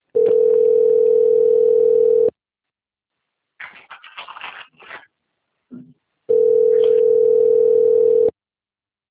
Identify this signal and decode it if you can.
Ringback tone (North America), 440 + 480 Hz.